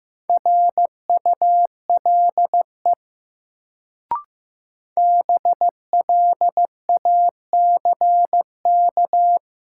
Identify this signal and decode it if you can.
Morse code, heard as a keyed tone, about 15 wpm, 700 Hz. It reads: RULE BLACK
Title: Sentences from Top 400 Words - New Word Drill - 15wpm